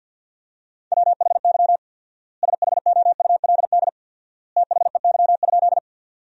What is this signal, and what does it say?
Morse code, heard as a keyed tone, about 53 wpm, 700 Hz. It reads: WHY SHOULD THEY?
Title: 4x Word-Spacing - Sentences from Top 300 Words - 50wpm